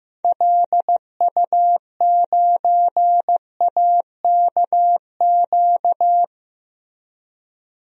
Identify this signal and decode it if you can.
Morse code, heard as a keyed tone, about 15 wpm, 700 Hz. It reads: LU9AKQ
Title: DXCC-100 Call Signs - 6 Characters - Repeated 3x - 15wpm